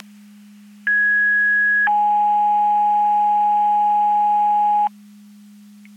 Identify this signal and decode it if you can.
Two-tone page, heard as a one-second tone, then a three-second tone, about 1650 then 830 Hz, with noise. It is heard from 0.9 to 4.9 s.
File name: Portage / 11-02 Portage 154.385 Paging